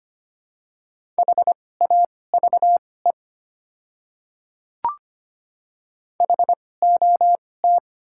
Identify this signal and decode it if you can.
Morse code, heard as a keyed tone, about 25 wpm, spaced as for 18 wpm, 700 Hz. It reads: HAVE HOT
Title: 2x Character Spacing - Top 100 Words - 25wpm